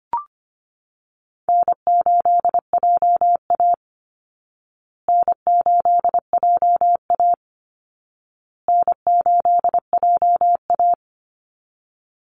Morse code, heard as a keyed tone, about 25 wpm, 700 Hz. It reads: N8JA N8JA N8JA
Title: Call Signs Repeated 2x - 25wpm